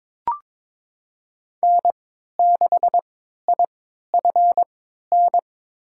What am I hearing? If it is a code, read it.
Morse code, heard as a keyed tone, about 22 wpm, spaced as for 13 wpm, 700 Hz. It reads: N6IFN
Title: USA Call Signs - 3x Character Spacing - 22wpm